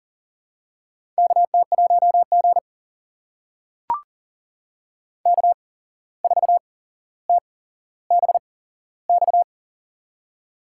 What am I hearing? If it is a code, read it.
Morse code, heard as a keyed tone, about 40 wpm, spaced as for 16 wpm, 700 Hz. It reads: KT1G K4TBX
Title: USA Call Signs - 8x Character Spacing - 40wpm